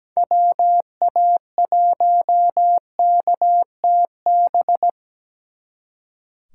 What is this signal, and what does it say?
Morse code, heard as a keyed tone, about 17 wpm, 700 Hz. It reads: WA1KTB
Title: Speed Racing - USA 2x3 Call Signs 17wpm